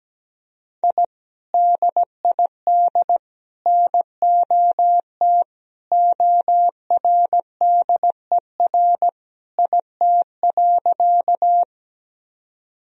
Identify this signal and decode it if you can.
Morse code, heard as a keyed tone, about 17 wpm, 700 Hz. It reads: I DID NOT ORDER IT.
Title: Sentences from Top 1000 Words - Review Part 2 - 17wpm